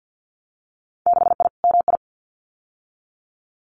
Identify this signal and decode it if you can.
Morse code, heard as a keyed tone, about 50 wpm, 700 Hz. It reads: 6I MI
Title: Winter Field Day Exchanges Encore 50wpm